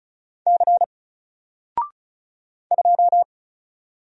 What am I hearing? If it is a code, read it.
Morse code, heard as a keyed tone, about 34 wpm, 700 Hz. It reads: C 2
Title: Single Letter-Number Alt 35wpm